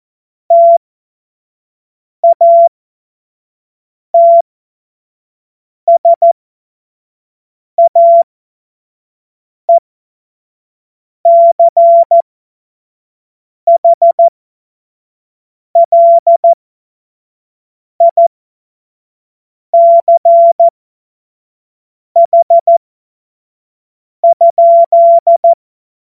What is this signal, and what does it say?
Morse code, heard as a keyed tone, about 14 wpm, spaced as for 5 wpm, 680 Hz. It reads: TATSAECHLICH?
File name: Morsekurs / Witze / Witz 20. Januar 2026 25 F70